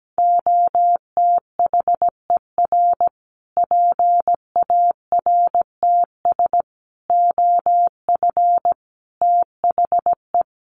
Morse code, heard as a keyed tone, about 17 wpm, 700 Hz. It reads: OTHER PARTS OF THE
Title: Sets of 5 Words Encore 17wpm